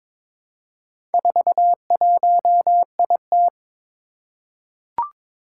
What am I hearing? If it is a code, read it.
Morse code, heard as a keyed tone, about 22 wpm, 700 Hz. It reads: 41IT
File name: practice / 22wpm / Four Letter-Number Combinations - 22wpm